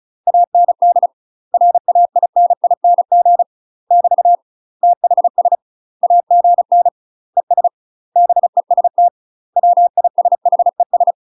Morse code, heard as a keyed tone, about 36 wpm, 700 Hz. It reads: AND RAINING <BT> THS AGN ES BEST WISHES
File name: UK MASTER QSOs TRAINING / 35WPM / UK MASTER QSOs 0012